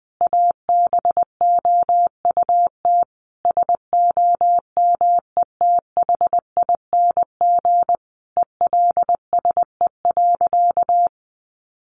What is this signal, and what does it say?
Morse code, heard as a keyed tone, about 19 wpm, 700 Hz. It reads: ABOUT SOMETHING ELSE.
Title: Sentences from Top 800 Words 20wpm